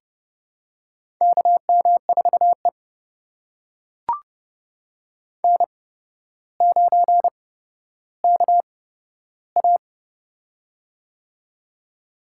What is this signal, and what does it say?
Morse code, heard as a keyed tone, about 32 wpm, spaced as for 13 wpm, 700 Hz. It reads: KM4E N9KA
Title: USA Call Signs - 4 Characters - 8x Character Spacing - 30wpm